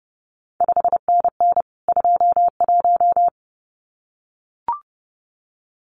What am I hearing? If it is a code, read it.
Morse code, heard as a keyed tone, about 30 wpm, 700 Hz. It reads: 5NN 21
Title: CQ WW DX Contest Exchange Encore - 30wpm